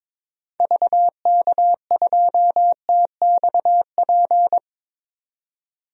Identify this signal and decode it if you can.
Morse code, heard as a keyed tone, about 22 wpm, 700 Hz. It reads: VK2TXP
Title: DXCC-100 Call Signs - 6 Characters - 22wpm